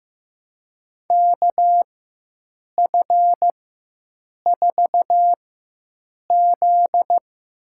Morse code, heard as a keyed tone, about 14 wpm, spaced as for 7 wpm, 700 Hz. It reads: KF4Z
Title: USA Call Signs - 4x Character Spacing - 15wpm